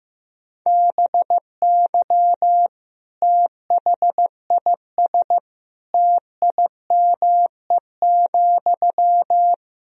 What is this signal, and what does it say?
Morse code, heard as a keyed tone, about 16 wpm, 700 Hz. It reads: BY THIS TIME,
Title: Sentences from Top 1000 Words - Review Part 4 - 15wpm